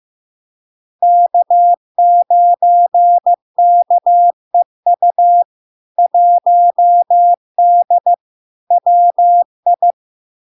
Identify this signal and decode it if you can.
Morse code, heard as a keyed tone, about 15 wpm, 700 Hz. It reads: K9KEU 1D WI
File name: LARK-FD-2022-Log-in-CW-15wpm-700-Hz